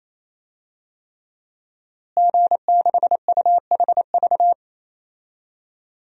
Morse code, heard as a keyed tone, about 28 wpm, 700 Hz. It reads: G6UHV
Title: DXCC-100 Call Signs - 5 Characters - Repeated 3x - 28wpm